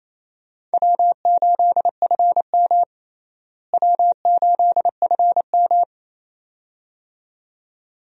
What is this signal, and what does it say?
Morse code, heard as a keyed tone, about 28 wpm, 700 Hz. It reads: W8FM W8FM
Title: USA Call Signs - 4 Characters - Repeated 2x - 28wpm